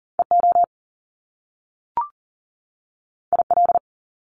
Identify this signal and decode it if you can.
Morse code, heard as a keyed tone, about 41 wpm, 700 Hz. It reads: EO IL
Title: Top Two Letter Combinations in Words 40wpm